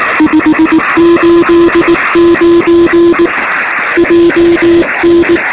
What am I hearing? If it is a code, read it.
Morse code, heard as a keyed tone, about 19 wpm, 340 Hz, with noise. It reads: 589 JN
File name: sm7lcb / 2046z ok2bfh 13cm